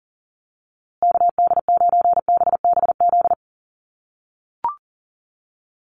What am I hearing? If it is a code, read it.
Morse code, heard as a keyed tone, about 40 wpm, 700 Hz. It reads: KD9BBZ
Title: USA Call Signs Repeated 3x 40wpm